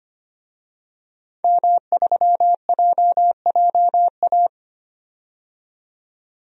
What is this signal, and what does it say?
Morse code, heard as a keyed tone, about 25 wpm, 700 Hz. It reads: M3JJA
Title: DXCC-100 Call Signs - 5 Characters - 25wpm